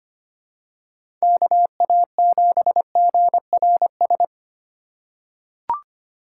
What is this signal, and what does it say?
Morse code, heard as a keyed tone, about 25 wpm, 700 Hz. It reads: KA7GRS
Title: USA Call Signs - 6 Characters - 4x Character Spacing - 25wpm